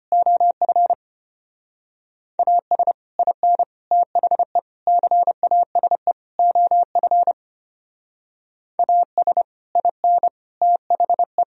Morse code, heard as a keyed tone, about 30 wpm, 700 Hz. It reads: OF AS IN THE CASE OF AS IN THE
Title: Speed-Racing - Sets of 5 Words 25wpm